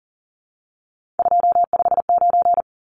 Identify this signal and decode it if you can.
Morse code, heard as a keyed tone, about 38 wpm, 700 Hz. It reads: 259
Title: Numbers - 3-Digit 40wpm